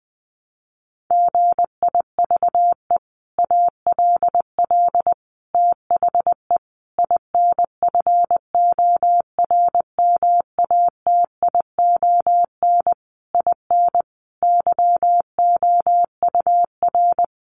Morse code, heard as a keyed tone, about 20 wpm, 700 Hz. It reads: GIVE ALL THE INFORMATION IN YOUR
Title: Sentences from Top 600 Words 20wpm